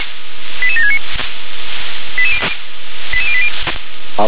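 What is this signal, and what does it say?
Five-tone selective call (ZVEI), heard as quick runs of five tones, tones of about 70 ms. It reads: 88769 88080
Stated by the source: ZVEI five-tone call received by SDR.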